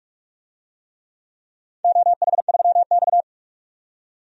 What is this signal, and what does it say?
Morse code, heard as a keyed tone, about 44 wpm, 700 Hz. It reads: OH3X
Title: DXCC-100 Call Signs - 4 Characters - 45wpm